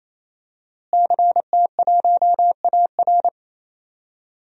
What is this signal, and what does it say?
Morse code, heard as a keyed tone, about 28 wpm, 700 Hz. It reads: CT1AR
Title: DXCC-100 Call Signs - 5 Characters - Repeated 3x - 28wpm